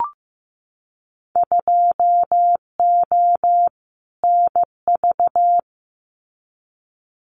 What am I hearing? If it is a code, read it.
Morse code, heard as a keyed tone, about 15 wpm, 700 Hz. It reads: 2O NV
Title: Winter Field Day Exchanges 15wpm